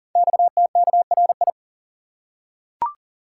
Morse code, heard as a keyed tone, about 38 wpm, 690 Hz. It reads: XTKRI
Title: Code Groups - 5 Letters - 40wpm